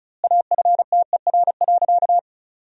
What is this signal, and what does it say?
Morse code, heard as a keyed tone, about 36 wpm, 700 Hz. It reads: AFTER.
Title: 2x Word-Spacing - Sentences from Top 500 Words - 35wpm